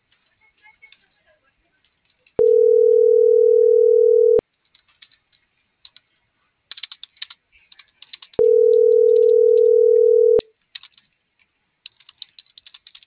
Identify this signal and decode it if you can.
Ringback tone (North America), 440 + 480 Hz.